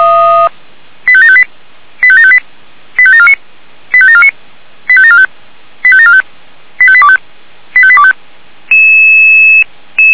Five-tone selective call (ZVEI), heard as quick runs of five tones, tones of about 70 ms. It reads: 85758 85758 85749 85749 85745 85745 85825 85825